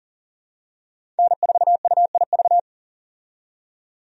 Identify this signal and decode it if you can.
Morse code, heard as a keyed tone, about 40 wpm, 700 Hz. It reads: N4UIV
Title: USA Call Signs - 5 Characters - 40wpm